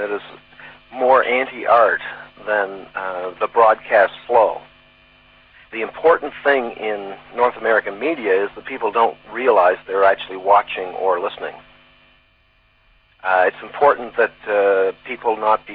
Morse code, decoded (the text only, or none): none